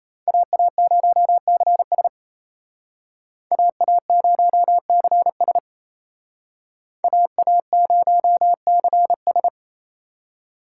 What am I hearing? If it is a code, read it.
Morse code, heard as a keyed tone, about 34 wpm, 700 Hz. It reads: AA0CS AA0CS AA0CS
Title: Speed Racing - USA 2x2 Call Signs 28wpm